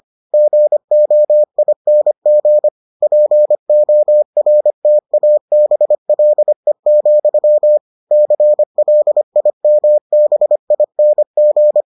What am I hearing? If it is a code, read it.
Morse code, heard as a keyed tone, about 25 wpm, 600 Hz. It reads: GOING PORTABLE, CLIMBING